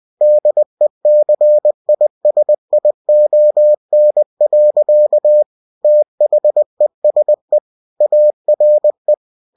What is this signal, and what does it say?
Morse code, heard as a keyed tone, about 20 wpm, 600 Hz. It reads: DECISION. THESE ARE
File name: cw-books / winnie-the-pooh / winnie-20wpm-page-0009